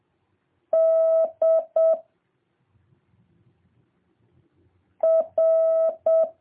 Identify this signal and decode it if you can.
Morse code, heard as a keyed tone, about 7 wpm, 650 Hz. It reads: D R